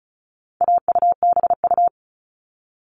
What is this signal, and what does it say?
Morse code, heard as a keyed tone, about 34 wpm, 700 Hz. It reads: AUBU
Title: All Cut Numbers - 4-Digit 35wpm